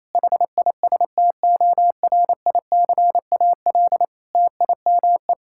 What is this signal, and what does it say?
Morse code, heard as a keyed tone, about 27 wpm, 700 Hz. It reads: HISTORICAL TIME
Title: Sentences from Top 1000 Words - Review Part 5 - 28wpm